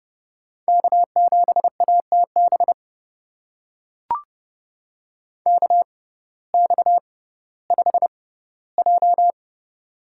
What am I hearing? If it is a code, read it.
Morse code, heard as a keyed tone, about 31 wpm, spaced as for 17 wpm, 700 Hz. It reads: K7ATB KX5J
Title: USA Call Signs - 6x Character Spacing - 30wpm